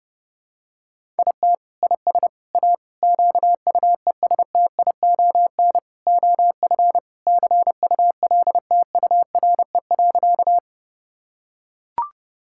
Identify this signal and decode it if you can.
Morse code, heard as a keyed tone, about 32 wpm, 700 Hz. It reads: IT IS A QUESTION OF CULTURE.
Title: Sentences from Top 1000 Words - Review Part 6 - 30wpm